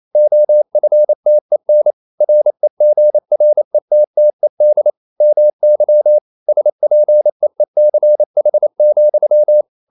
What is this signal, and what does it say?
Morse code, heard as a keyed tone, about 28 wpm, 600 Hz. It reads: OFTEN REGRETTED MY SPEECH,
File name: CW / Practice / EnglishQuotations0228